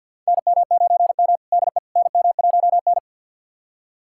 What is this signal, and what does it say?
Morse code, heard as a keyed tone, about 53 wpm, 700 Hz. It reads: NM9M DE NM1N
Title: USA Callsign DE Callsign - 4 Characters - 50wpm